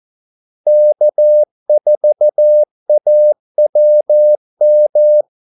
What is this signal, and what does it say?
Morse code, heard as a keyed tone, about 14 wpm, 600 Hz. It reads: K4AWM
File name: Callsigns-14wpm